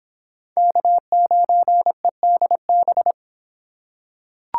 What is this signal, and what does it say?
Morse code, heard as a keyed tone, about 25 wpm, 700 Hz. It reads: K9EDB